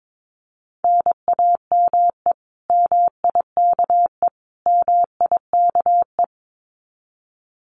Morse code, heard as a keyed tone, about 22 wpm, 700 Hz. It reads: NAME MIKE MIKE
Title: Common QSO Phrases Encore 22wpm